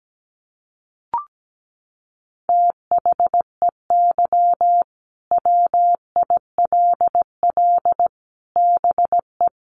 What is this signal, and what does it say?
Morse code, heard as a keyed tone, about 17 wpm, 700 Hz. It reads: THEY WILL BE
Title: Sets of 5 Words Encore 17wpm